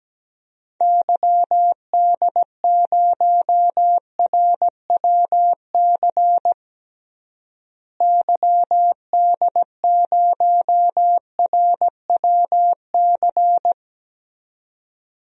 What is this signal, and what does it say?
Morse code, heard as a keyed tone, about 17 wpm, 700 Hz. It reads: YD0RWC YD0RWC
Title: DXCC-100 Call Signs - 6 Characters - Repeated 2x - 17wpm